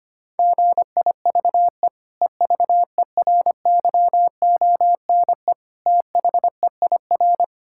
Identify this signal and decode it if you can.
Morse code, heard as a keyed tone, about 25 wpm, 700 Hz. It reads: GIVE EVERYONE THEIR
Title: Sentences from Top 1000 Words - Review Part 6 - 25wpm